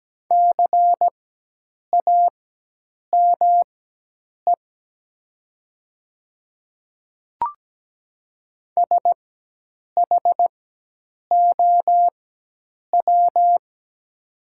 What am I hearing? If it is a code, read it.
Morse code, heard as a keyed tone, about 17 wpm, spaced as for 8 wpm, 700 Hz. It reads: CAME SHOW
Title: 4x Character Spacing - Top 200 Words - 17wpm